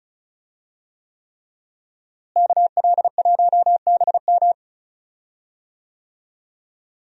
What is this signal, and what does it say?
Morse code, heard as a keyed tone, about 36 wpm, 700 Hz. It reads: KL1BM
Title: USA Call Signs - 5 Characters - Repeated 3x - 35wpm